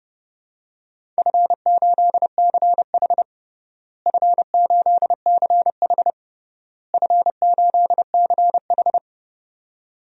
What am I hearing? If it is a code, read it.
Morse code, heard as a keyed tone, about 30 wpm, 700 Hz. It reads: F8CH F8CH F8CH
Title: DXCC-100 Call Signs - 4 Characters - Repeated 3x - 30wpm